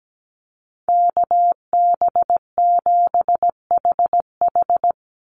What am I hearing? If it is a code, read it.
Morse code, heard as a keyed tone, about 17 wpm, 700 Hz. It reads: KB7HH
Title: NA SOTA Chaser Callsigns 17wpm